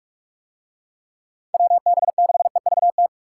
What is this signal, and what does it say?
Morse code, heard as a keyed tone, about 41 wpm, 700 Hz. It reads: WB6EVT